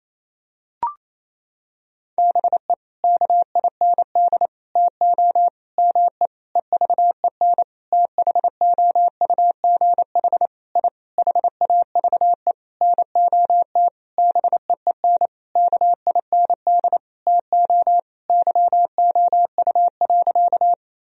Morse code, heard as a keyed tone, about 28 wpm, 700 Hz. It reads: BE KIND TO ME EVEN THOUGH I HAVE NOT BEEN KIND TO YOU.
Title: Sentences from Top 1000 Words - Review Part 2 - 28wpm